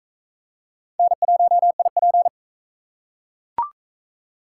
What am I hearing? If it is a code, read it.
Morse code, heard as a keyed tone, about 40 wpm, 700 Hz. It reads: N1IP